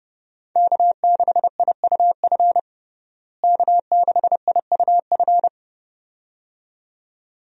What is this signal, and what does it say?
Morse code, heard as a keyed tone, about 32 wpm, 700 Hz. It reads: K6IUF K6IUF
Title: USA Call Signs - 5 Characters - Repeated 2x - 30wpm